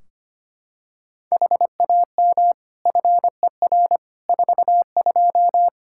Morse code, heard as a keyed tone, about 25 wpm, 700 Hz. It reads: HAM FER 42
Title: Practice QSO - Exchange 3 - More Info - 25wpm